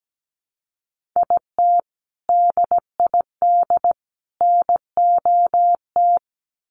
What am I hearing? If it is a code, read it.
Morse code, heard as a keyed tone, about 17 wpm, 700 Hz. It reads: IT DID NOT
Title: Sets of 3 Words 17wpm